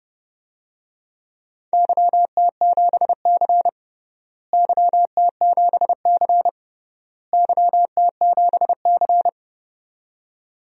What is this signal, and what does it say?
Morse code, heard as a keyed tone, about 30 wpm, 700 Hz. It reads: YT7C YT7C YT7C
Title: DXCC-100 Call Signs - 4 Characters - Repeated 3x - 30wpm